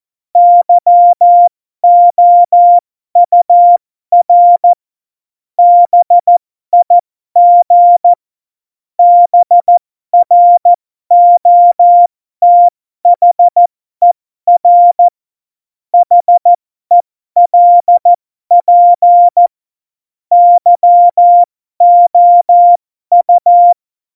Morse code, heard as a keyed tone, about 14 wpm, spaced as for 12 wpm, 700 Hz. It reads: YOUR BIG BROTHER HELP YOU
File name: Morsekurs / Witze